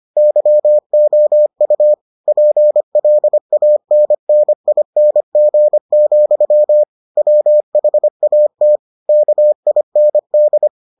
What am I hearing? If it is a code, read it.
Morse code, heard as a keyed tone, about 25 wpm, 600 Hz. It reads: YOU PLANNING, WHAT KIND